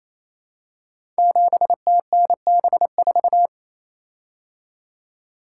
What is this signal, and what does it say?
Morse code, heard as a keyed tone, about 27 wpm, 700 Hz. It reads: 7TNB4
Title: Code Groups - 5 Letter:Numbers - 28wpm